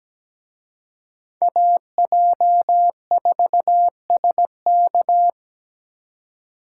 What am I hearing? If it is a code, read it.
Morse code, heard as a keyed tone, about 20 wpm, 700 Hz. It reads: AJ4SK
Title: Speed Racing - USA Call Signs 17wpm